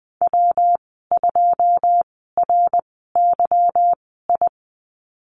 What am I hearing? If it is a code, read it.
Morse code, heard as a keyed tone, about 20 wpm, spaced as for 15 wpm, 700 Hz. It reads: W2RYI
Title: 154 Lesson 25g - Letter p - Call Signs 20wpm